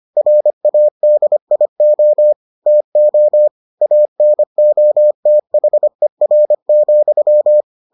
Morse code, heard as a keyed tone, about 25 wpm, 600 Hz. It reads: RADIO TO ANOTHER,